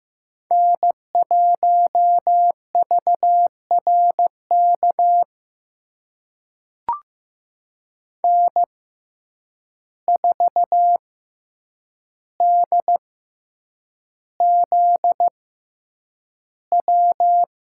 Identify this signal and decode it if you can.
Morse code, heard as a keyed tone, about 15 wpm, spaced as for 8 wpm, 700 Hz. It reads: N1VRK N4DZW
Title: USA Call Signs - 5 Characters - 6x Character Spacing - 15wpm